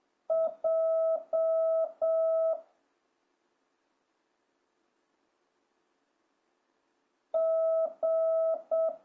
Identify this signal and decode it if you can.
Morse code, heard as a keyed tone, about 7 wpm, 1300 Hz. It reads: J G